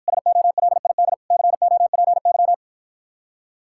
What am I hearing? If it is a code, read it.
Morse code, heard as a keyed tone, about 54 wpm, 700 Hz. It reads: SOLID COPY